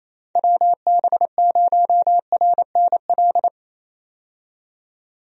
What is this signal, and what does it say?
Morse code, heard as a keyed tone, about 28 wpm, 700 Hz. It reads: WB0RNL
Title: USA Call Signs - 6 Characters - 28wpm